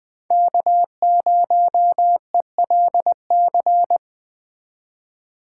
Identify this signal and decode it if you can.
Morse code, heard as a keyed tone, about 20 wpm, 700 Hz. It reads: K0ELC
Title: Speed Racing - USA Call Signs 17wpm